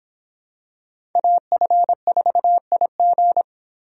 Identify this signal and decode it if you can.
Morse code, heard as a keyed tone, about 25 wpm, 700 Hz. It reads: AF4IG